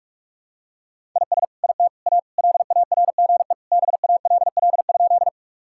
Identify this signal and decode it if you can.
Morse code, heard as a keyed tone, about 41 wpm, 700 Hz. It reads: IS IT A LARGE BALL?
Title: Sentences from Top 1000 Words - Review Part 6 - 45wpm